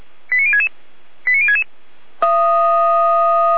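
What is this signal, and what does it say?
Five-tone selective call (ZVEI), heard as quick runs of five tones, tones of about 70 ms. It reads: 89066 89066
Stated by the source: ZVEI five-tone selective call 89105, received by SDR.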